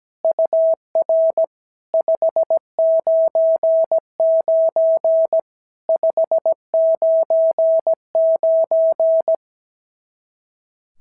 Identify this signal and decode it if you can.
Morse code, heard as a keyed tone, about 16 wpm, 650 Hz. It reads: UR 599 599
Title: SOTA QSO Practice 17wpm